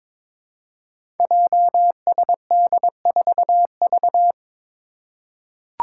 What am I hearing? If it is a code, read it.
Morse code, heard as a keyed tone, about 22 wpm, 700 Hz. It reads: JSD4V
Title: Five Challenging Letter-Number Combinations 22wpm